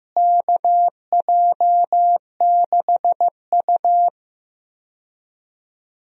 Morse code, heard as a keyed tone, about 15 wpm, 700 Hz. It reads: KJ6U
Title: USA Call Signs - 4 Characters - Repeated 2x - 15wpm